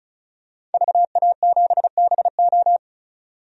Morse code, heard as a keyed tone, about 36 wpm, 700 Hz. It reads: VA7BO
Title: DXCC-100 Call Signs - 5 Characters - 35wpm